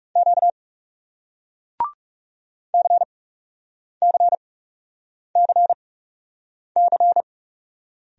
Morse code, heard as a keyed tone, about 39 wpm, spaced as for 8 wpm, 700 Hz. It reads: Q CCCC